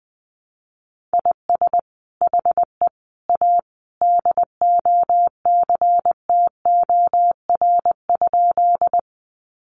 Morse code, heard as a keyed tone, about 19 wpm, 700 Hz. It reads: IS HE A DOCTOR?
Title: Sentences from Top 1500 Words 20wpm